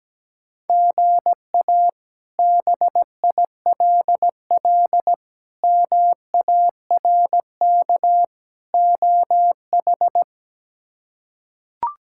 Morse code, heard as a keyed tone, about 17 wpm, 700 Hz. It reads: GA BILL MARK OH
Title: K1USN SST QSO Practice - 17wpm